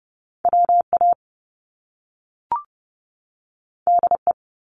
Morse code, heard as a keyed tone, about 30 wpm, 700 Hz. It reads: WA DE
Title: US States 30wpm